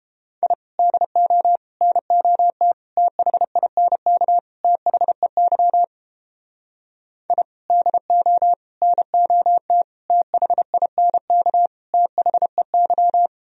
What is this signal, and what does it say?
Morse code, heard as a keyed tone, about 29 wpm, 700 Hz. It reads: I DO NOT THINK THEY I DO NOT THINK THEY